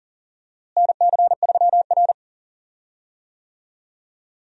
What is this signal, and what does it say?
Morse code, heard as a keyed tone, about 40 wpm, 700 Hz. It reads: NC3R
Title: USA Call Signs - 4 Characters - Repeated 2x - 40wpm